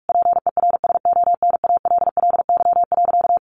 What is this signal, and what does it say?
Morse code, heard as a keyed tone, about 44 wpm, 700 Hz. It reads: PERSONALLY.
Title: Sentences from Top 1400 Words 45wpm